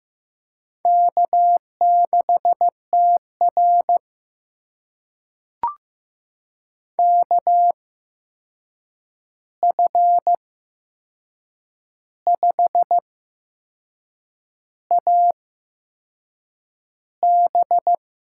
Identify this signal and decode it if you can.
Morse code, heard as a keyed tone, about 15 wpm, spaced as for 6 wpm, 700 Hz. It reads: K6TR KF5AB
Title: USA Call Signs - 8x Character Spacing - 15wpm